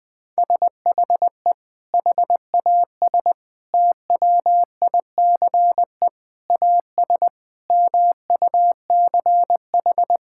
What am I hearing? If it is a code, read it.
Morse code, heard as a keyed tone, about 21 wpm, 700 Hz. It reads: SHE HAS TWICE AS MUCH
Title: Sentences from Top 1600 Words 20wpm